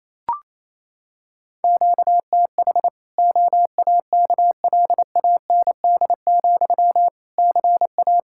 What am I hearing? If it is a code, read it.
Morse code, heard as a keyed tone, about 28 wpm, 700 Hz. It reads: QTH OAKLAND, CA
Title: Practice QSO - Exchange 1 - Beginning - 28wpm